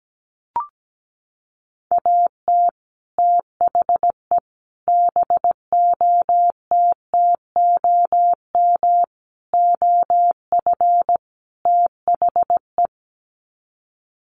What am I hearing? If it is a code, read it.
Morse code, heard as a keyed tone, about 17 wpm, 700 Hz. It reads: AT THE BOTTOM OF THE
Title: Sets of 5 Words 17wpm